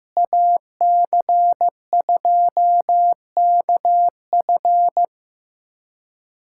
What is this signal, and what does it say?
Morse code, heard as a keyed tone, about 15 wpm, 700 Hz. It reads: AC2KF
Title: USA Callsign DE Callsign - 5 Characters - 15wpm